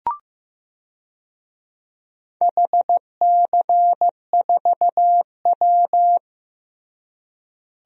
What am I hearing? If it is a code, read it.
Morse code, heard as a keyed tone, about 15 wpm, 700 Hz. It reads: HC4W
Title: DXCC-100 Call Signs - 4 Characters - Repeated 3x - 15wpm